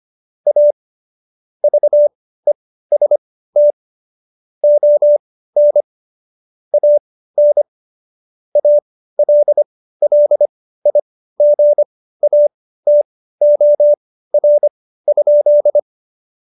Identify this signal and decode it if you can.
Morse code, heard as a keyed tone, about 25 wpm, spaced as for 15 wpm, 590 Hz. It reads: A VEST ON AN ALLIGATOR?
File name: Jokes-25-char-15-eff